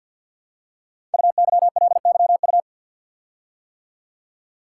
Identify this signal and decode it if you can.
Morse code, heard as a keyed tone, about 46 wpm, 700 Hz. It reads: UYLYU
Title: Code Groups - 5 Letters - 50wpm